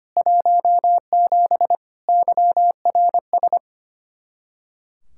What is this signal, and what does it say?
Morse code, heard as a keyed tone, about 25 wpm, 700 Hz. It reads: 17 YRS
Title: Practice QSO - Exchange 3 - More Info - 25wpm